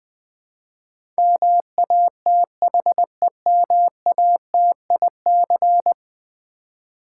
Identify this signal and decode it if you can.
Morse code, heard as a keyed tone, about 20 wpm, 700 Hz. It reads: MATHEMATIC
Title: Build A Word - 5 Letters At A Time - 12-Letter Words - 20wpm